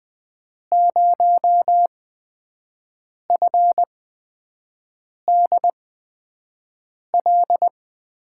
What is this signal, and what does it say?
Morse code, heard as a keyed tone, about 20 wpm, spaced as for 5 wpm, 700 Hz. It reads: 0FDL
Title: USA Call Signs - 6 Characters - 8x Character Spacing - 20wpm